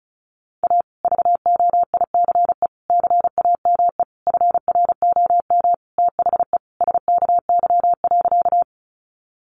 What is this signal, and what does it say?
Morse code, heard as a keyed tone, about 34 wpm, 700 Hz. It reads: A VOICE CAME FROM THE SKY.
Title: Sentences from Top 1200 Words 35wpm